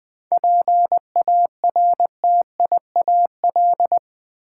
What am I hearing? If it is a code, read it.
Morse code, heard as a keyed tone, about 20 wpm, 700 Hz. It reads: PARTIAL
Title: 7 Letter Words - First 3 Letters Given - 20wpm